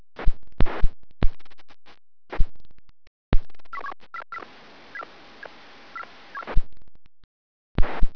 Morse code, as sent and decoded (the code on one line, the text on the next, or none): none